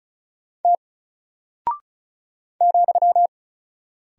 Morse code, heard as a keyed tone, about 34 wpm, 700 Hz. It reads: T ,
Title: Single Character (Letter:Number:Punctuation:Prosign) 35wpm